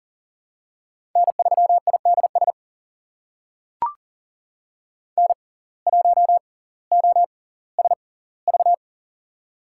Morse code, heard as a keyed tone, about 40 wpm, spaced as for 20 wpm, 700 Hz. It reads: N3IDS N1OSV
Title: USA Call Signs - 5 Characters - 6x Character Spacing - 40wpm